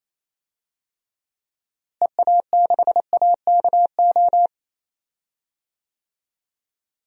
Morse code, heard as a keyed tone, about 28 wpm, 700 Hz. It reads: EA6AKO
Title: DXCC-100 Call Signs - 6 Characters - Repeated 3x - 28wpm